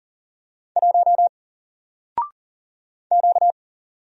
Morse code, heard as a keyed tone, about 39 wpm, 700 Hz. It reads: 1 Q
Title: Single Character (Letter:Number:Punctuation:Prosign) 40wpm